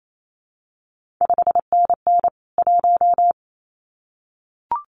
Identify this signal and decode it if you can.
Morse code, heard as a keyed tone, about 28 wpm, 700 Hz. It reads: 5NN 1
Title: CQ WW DX Contest Exchange Encore - 28wpm